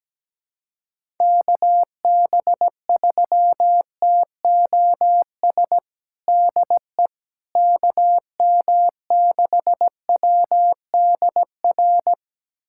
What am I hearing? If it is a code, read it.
Morse code, heard as a keyed tone, about 17 wpm, 700 Hz. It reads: KB3TOS DE KM6WDR
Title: USA Callsign DE Callsign - 6 Characters - 17wpm